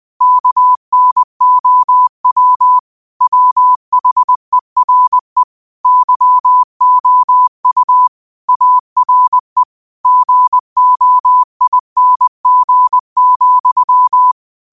Morse code, heard as a keyed tone, about 20 wpm, 1000 Hz. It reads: KNOW WHERE YOU ARE GOING,
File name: qotdfiles / 20wpm